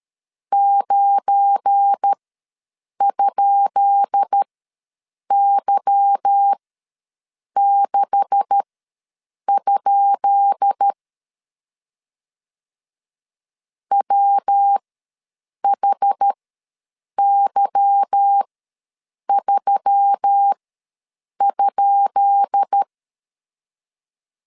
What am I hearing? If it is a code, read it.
Morse code, heard as a keyed tone, about 13 wpm, spaced as for 7 wpm, 800 Hz. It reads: 9?Y6? WHY3?
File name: LEZIONI / 10 SIGNS / 10B INT ws 12wpm 3527